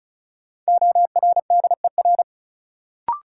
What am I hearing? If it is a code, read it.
Morse code, heard as a keyed tone, about 36 wpm, 700 Hz. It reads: ORDER
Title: Sentences from Top 400 Words - New Word Drill - 35wpm